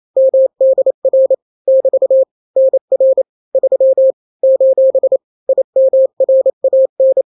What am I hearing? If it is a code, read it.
Morse code, heard as a keyed tone, about 28 wpm, 530 Hz. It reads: MDR <BT> NR 3 : IMRAN
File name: daily / 27wpm